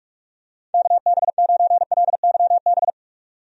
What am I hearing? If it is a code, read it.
Morse code, heard as a keyed tone, about 46 wpm, 700 Hz. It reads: KB9LYB
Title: USA Call Signs - 6 Characters - 45wpm